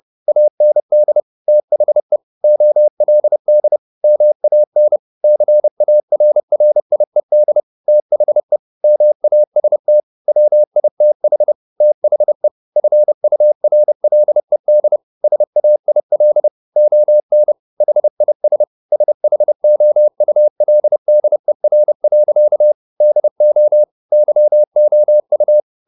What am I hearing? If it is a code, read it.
Morse code, heard as a keyed tone, about 30 wpm, 600 Hz. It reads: AND THE OLD MAN CARRIED THE MAST WITH THE FURLED SAIL ON HIS SHOULDER. DO YOU
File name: the old man and the sea / the old man and the sea 1 30wpm